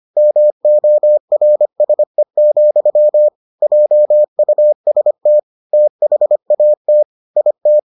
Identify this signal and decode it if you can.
Morse code, heard as a keyed tone, about 25 wpm, 600 Hz. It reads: MORSE, JUST THAT IT